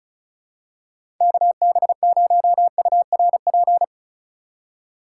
Morse code, heard as a keyed tone, about 36 wpm, 700 Hz. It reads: KB0URP
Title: USA Call Signs - 6 Characters - 3x Character Spacing - 35wpm